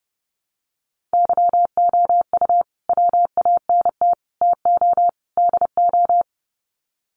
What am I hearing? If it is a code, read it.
Morse code, heard as a keyed tone, about 30 wpm, 700 Hz. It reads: YOU WANT TO DO
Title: Sets of 4 Words 30wpm